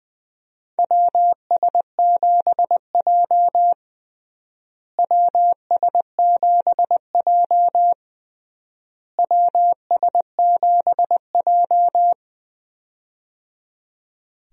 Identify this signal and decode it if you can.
Morse code, heard as a keyed tone, about 20 wpm, 700 Hz. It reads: WS7J WS7J WS7J
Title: USA Call Signs - 4 Characters - Repeated 3x - 20wpm